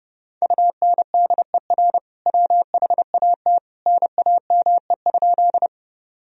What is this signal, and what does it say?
Morse code, heard as a keyed tone, about 30 wpm, 700 Hz. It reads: UNDER WHAT NAME?
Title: Sentences from Top 1000 Words - Review Part 1 - 30wpm